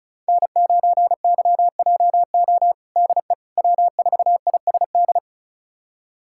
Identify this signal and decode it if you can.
Morse code, heard as a keyed tone, about 36 wpm, 700 Hz. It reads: N9YJO DE W4ISD
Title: USA Callsign DE Callsign - 5 Characters - 35wpm